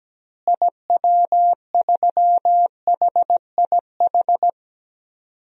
Morse code, heard as a keyed tone, about 17 wpm, 700 Hz. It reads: IW3HIH
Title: DXCC-100 Call Signs - 6 Characters - Repeated 2x - 17wpm